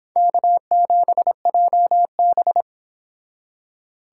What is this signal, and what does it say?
Morse code, heard as a keyed tone, about 31 wpm, 700 Hz. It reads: K7JB